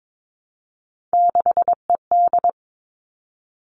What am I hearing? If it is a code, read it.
Morse code, heard as a keyed tone, about 22 wpm, 700 Hz. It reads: 6ED
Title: All Cut Numbers - 3-Digit 22wpm